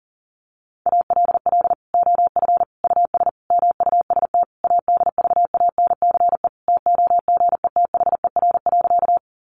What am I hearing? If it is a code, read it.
Morse code, heard as a keyed tone, about 40 wpm, 700 Hz. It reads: ALL OF US MUST ADVANCE TOGETHER.
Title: Sentences from Top 1400 Words 40wpm